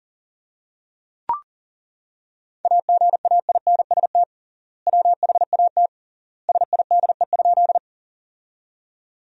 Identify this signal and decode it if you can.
Morse code, heard as a keyed tone, about 40 wpm, 700 Hz. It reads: AGAINST WHAT SIDE?
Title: 3x Word-Spacing - Sentences from Top 400 Words - 40wpm